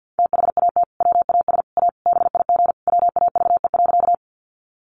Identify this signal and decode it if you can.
Morse code, heard as a keyed tone, about 50 wpm, 700 Hz. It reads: THAT WAS A BIG WAVE.
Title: Sentences from Top 1400 Words 50wpm